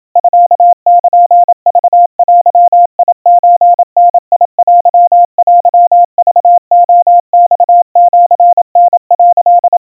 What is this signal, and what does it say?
Morse code (Japanese Wabun code), heard as a keyed tone, about 27 wpm, 700 Hz. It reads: ミルクデソダテテクレマシタ」
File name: kawa / Morse sample / kamyu 100 cpm 700 Hz Crispy